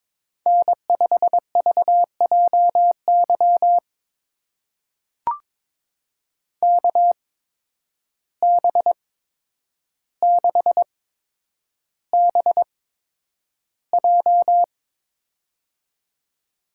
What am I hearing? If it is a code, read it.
Morse code, heard as a keyed tone, about 22 wpm, spaced as for 9 wpm, 700 Hz. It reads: N5VJY KB6BJ
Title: USA Call Signs - 5 Characters - 8x Character Spacing - 22wpm